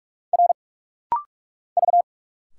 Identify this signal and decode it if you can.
Morse code, heard as a keyed tone, about 46 wpm, 700 Hz. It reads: R V
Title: Single Letters - Rapid-Fire 45wpm